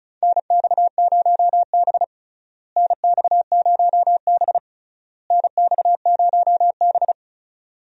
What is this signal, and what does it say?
Morse code, heard as a keyed tone, about 36 wpm, 700 Hz. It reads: NX0B NX0B NX0B
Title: USA Call Signs - 4 Characters - Repeated 3x - 35wpm